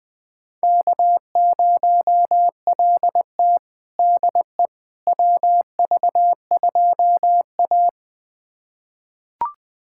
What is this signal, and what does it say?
Morse code, heard as a keyed tone, about 20 wpm, 700 Hz. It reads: K0LT DE WV2A
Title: USA Callsign DE Callsign - 4 Characters - 20wpm